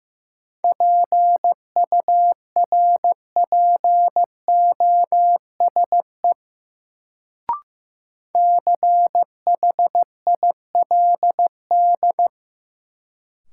Morse code, heard as a keyed tone, about 15 wpm, 700 Hz. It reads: PURPOSE CHILD
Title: Sentences from Top 800 Words - New Word Drill - 15wpm